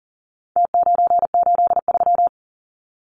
Morse code, heard as a keyed tone, about 40 wpm, 700 Hz. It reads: T983
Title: Numbers - Contest Style - 4-Digit 40wpm